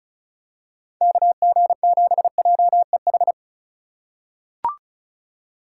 Morse code, heard as a keyed tone, about 36 wpm, 700 Hz. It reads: KG7JEH
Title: USA Call Signs - 6 Characters - 8x Character Spacing - 35wpm